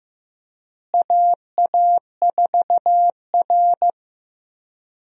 Morse code, heard as a keyed tone, about 15 wpm, 700 Hz. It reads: AA4R
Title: USA Call Signs - 4 Characters - Repeated 2x - 15wpm